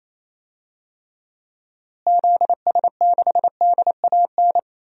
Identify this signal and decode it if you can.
Morse code, heard as a keyed tone, about 28 wpm, 700 Hz. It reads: ZS6DAN
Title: DXCC-100 Call Signs - 6 Characters - Repeated 3x - 28wpm